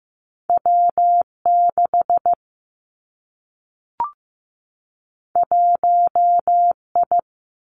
Morse code, heard as a keyed tone, about 15 wpm, 700 Hz. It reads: W 6 1 I
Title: Two Challenging Letter-Number Combinations 15wpm